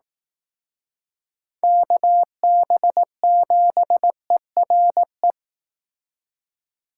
Morse code, heard as a keyed tone, about 18 wpm, 700 Hz. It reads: KB7ERE
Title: Speed Racing - USA 2x3 Call Signs 15wpm